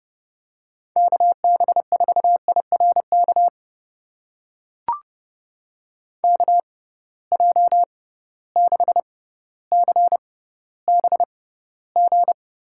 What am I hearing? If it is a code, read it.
Morse code, heard as a keyed tone, about 31 wpm, spaced as for 16 wpm, 700 Hz. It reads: KB4IRK KJ6CBG
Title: USA Call Signs - 6 Characters - 6x Character Spacing - 30wpm